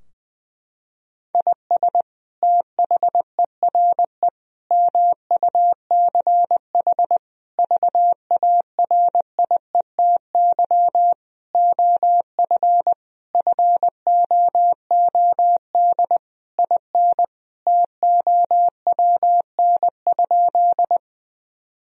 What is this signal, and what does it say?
Morse code, heard as a keyed tone, about 21 wpm, 700 Hz. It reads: IS THERE MUCH VARIETY OF FOOD IN TOWN?
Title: Sentences from Top 1000 Words - Review Part 4 - 20wpm